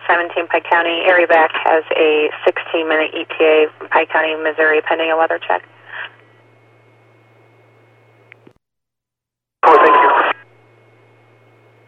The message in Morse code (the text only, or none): none